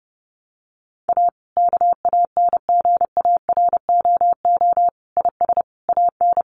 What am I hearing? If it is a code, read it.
Morse code, heard as a keyed tone, about 30 wpm, 700 Hz. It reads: A KANGAROO IS AN
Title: Easy Phrases 30wpm